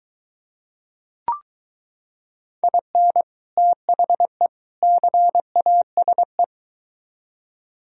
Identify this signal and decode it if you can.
Morse code, heard as a keyed tone, about 23 wpm, 700 Hz. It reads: IN THE CASE